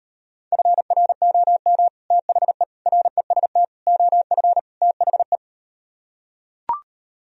Morse code, heard as a keyed tone, about 39 wpm, 700 Hz. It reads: FROM THE REST OF THE